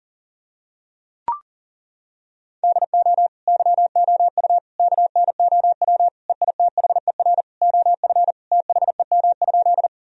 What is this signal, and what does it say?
Morse code, heard as a keyed tone, about 40 wpm, 700 Hz. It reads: DO YOU KNOW EITHER OF THEM?
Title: Sentences from Top 1000 Words - Review Part 4 - 40wpm